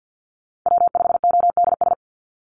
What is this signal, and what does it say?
Morse code, heard as a keyed tone, about 50 wpm, 700 Hz. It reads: W5ODS
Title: NA SOTA Activator Callsigns 50wpm